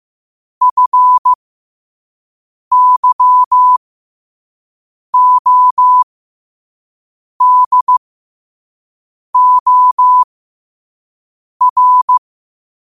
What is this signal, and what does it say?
Morse code, heard as a keyed tone, about 15 wpm, spaced as for 5 wpm, 1000 Hz. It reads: FYODOR
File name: qotdfiles / 5wpm